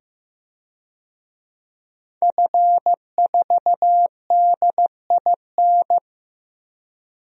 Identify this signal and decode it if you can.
Morse code, heard as a keyed tone, about 15 wpm, 700 Hz. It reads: F4DIN
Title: DXCC-100 Call Signs - 5 Characters - 15wpm